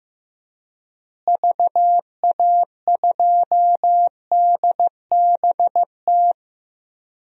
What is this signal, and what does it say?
Morse code, heard as a keyed tone, about 15 wpm, 700 Hz. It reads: VA2DBT
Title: DXCC-100 Call Signs - 6 Characters - Repeated 2x - 15wpm